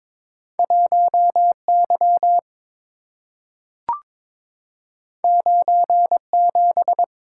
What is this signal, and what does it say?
Morse code, heard as a keyed tone, about 22 wpm, 700 Hz. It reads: 1Y 97
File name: practice / 22wpm / Two Letter-Number Combinations - 22wpm